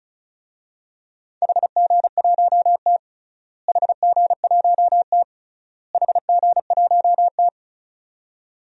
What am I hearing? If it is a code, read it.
Morse code, heard as a keyed tone, about 36 wpm, 700 Hz. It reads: HG1T HG1T HG1T
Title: DXCC-100 Call Signs - 4 Characters - Repeated 3x - 35wpm